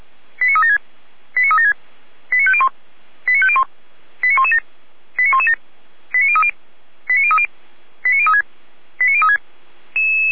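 Five-tone selective call (ZVEI), heard as quick runs of five tones, tones of about 70 ms. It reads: 89376 89376 89501 89501 89117 89117 89030 89030 89036 89036
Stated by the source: ZVEI five-tone selective call, received on SDR.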